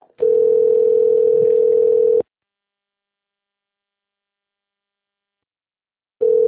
Ringback tone (North America), 440 + 480 Hz.